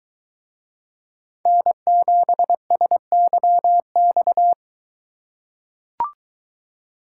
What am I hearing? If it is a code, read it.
Morse code, heard as a keyed tone, about 22 wpm, 700 Hz. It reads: N7SYX